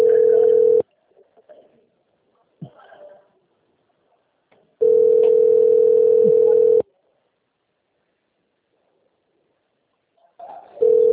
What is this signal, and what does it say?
Ringback tone (North America), 440 + 480 Hz.